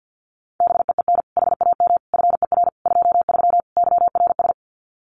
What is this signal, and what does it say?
Morse code, heard as a keyed tone, about 50 wpm, 700 Hz. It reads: BEEN HAM FER 23 YRS
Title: Common QSO Phrases Encore 50wpm